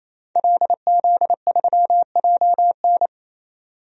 Morse code, heard as a keyed tone, about 28 wpm, 700 Hz. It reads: LZ3JN
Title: DXCC-100 Call Signs - 5 Characters - 28wpm